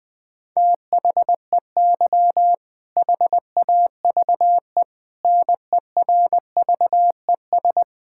Morse code, heard as a keyed tone, about 20 wpm, 700 Hz. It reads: THEY HAVE NERVES
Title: All-Age Appropriate Jokes - 20wpm